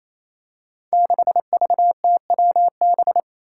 Morse code, heard as a keyed tone, about 28 wpm, 700 Hz. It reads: 6VTWB
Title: Five Challenging Letter-Number Combinations 28wpm